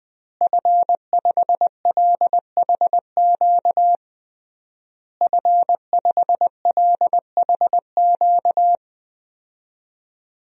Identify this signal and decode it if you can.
Morse code, heard as a keyed tone, about 21 wpm, 700 Hz. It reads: F5LHQ F5LHQ
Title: DXCC-100 Call Signs - 5 Characters - Repeated 3x - 20wpm